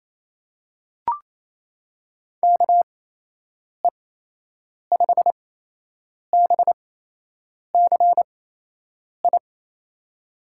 Morse code, heard as a keyed tone, about 27 wpm, spaced as for 8 wpm, 700 Hz. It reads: KE5BCI
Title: USA Call Signs - 6 Characters - 8x Character Spacing - 28wpm